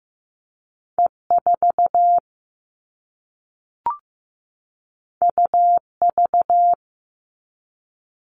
Morse code, heard as a keyed tone, about 15 wpm, 700 Hz. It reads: E4 UV
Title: All Cut Numbers - 2-Digit 15wpm